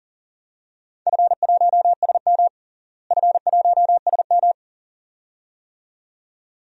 Morse code, heard as a keyed tone, about 40 wpm, 700 Hz. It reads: F1SM F1SM
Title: DXCC-100 Call Signs - 4 Characters - Repeated 2x - 40wpm